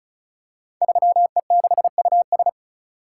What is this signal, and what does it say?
Morse code, heard as a keyed tone, about 36 wpm, 700 Hz. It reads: 3E6US
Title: Code Groups - 5 Letter:Numbers - 35wpm